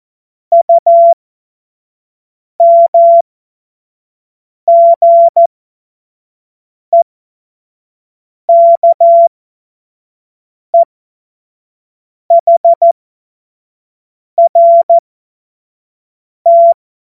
Morse code, heard as a keyed tone, about 14 wpm, spaced as for 5 wpm, 680 Hz. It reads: UMGEKEHRT
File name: 25-F70